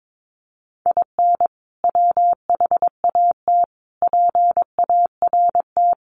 Morse code, heard as a keyed tone, about 22 wpm, 700 Hz. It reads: IN WHAT PART
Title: Sentences from Top 200 Words 22wpm